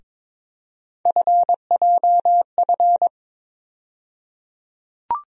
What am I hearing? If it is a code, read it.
Morse code, heard as a keyed tone, about 22 wpm, 700 Hz. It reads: FJF
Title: USA Call Sign Postfixes - 3 Letters 22wpm